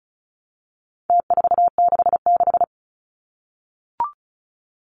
Morse code, heard as a keyed tone, about 34 wpm, 700 Hz. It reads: T466
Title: Numbers - Contest Style - 4-Digit 35wpm